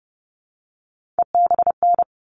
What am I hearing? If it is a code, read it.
Morse code, heard as a keyed tone, about 30 wpm, 700 Hz. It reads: EBN
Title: All Cut Numbers - 3-Digit 30wpm